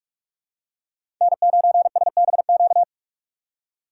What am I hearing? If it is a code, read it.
Morse code, heard as a keyed tone, about 44 wpm, 700 Hz. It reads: N9SBQ
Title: USA Call Signs - 6x Character Spacing - 45wpm